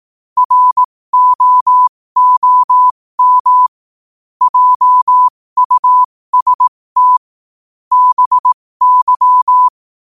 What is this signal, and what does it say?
Morse code, heard as a keyed tone, about 18 wpm, spaced as for 15 wpm, 1000 Hz. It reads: ROOM JUST BY
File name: qotdfiles / 15wpm